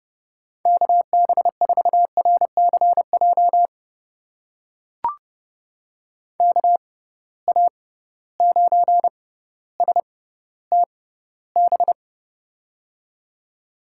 Morse code, heard as a keyed tone, about 30 wpm, spaced as for 15 wpm, 700 Hz. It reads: KB4RCJ KA9STB
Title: USA Call Signs - 6 Characters - 6x Character Spacing - 30wpm